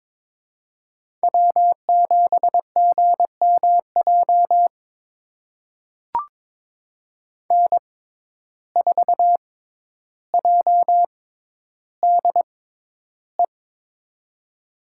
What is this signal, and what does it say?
Morse code, heard as a keyed tone, about 21 wpm, spaced as for 11 wpm, 700 Hz. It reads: W7GMJ N4JDE
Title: USA Call Signs - 5 Characters - 6x Character Spacing - 22wpm